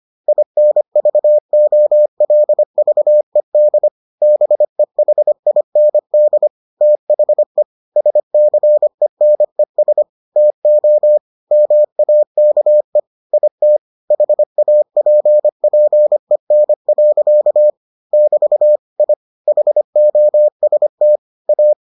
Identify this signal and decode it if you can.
Morse code, heard as a keyed tone, about 25 wpm, 600 Hz. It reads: INVOLVED BEHIND THE SCENES TO MAKE IT HAPPEN. <BT> I HOST A